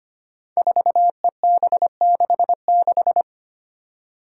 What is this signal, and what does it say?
Morse code, heard as a keyed tone, about 25 wpm, 700 Hz. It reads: 4EB66
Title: Five Challenging Letter-Number Combinations 25wpm